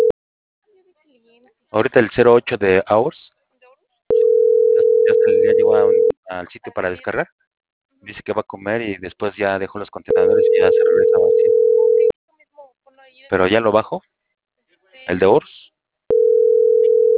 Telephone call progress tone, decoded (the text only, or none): ringback tone